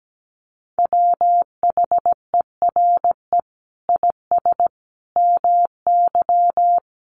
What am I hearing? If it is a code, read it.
Morse code, heard as a keyed tone, about 17 wpm, 700 Hz. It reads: WHERE IS MY
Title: Sentences from Top 1400 Words 17wpm